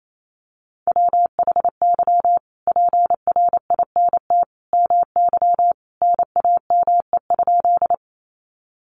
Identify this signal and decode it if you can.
Morse code, heard as a keyed tone, about 28 wpm, 700 Hz. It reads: WHY PRINT MY NAME?
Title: Sentences from Top 1400 Words 28wpm